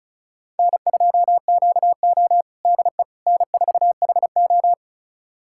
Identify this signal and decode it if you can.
Morse code, heard as a keyed tone, about 34 wpm, 700 Hz. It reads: N2QO DE N4HO
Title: USA Callsign DE Callsign - 4 Characters - 35wpm